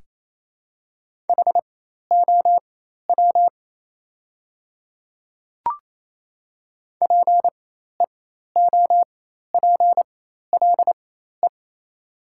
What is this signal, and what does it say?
Morse code, heard as a keyed tone, about 27 wpm, spaced as for 13 wpm, 700 Hz. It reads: HOW PEOPLE
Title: 4x Character Spacing - Top 100 Words - 28wpm